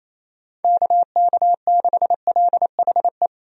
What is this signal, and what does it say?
Morse code, heard as a keyed tone, about 28 wpm, 700 Hz. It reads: KK6LHE
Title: USA Call Signs - 6 Characters - Repeated 2x - 28wpm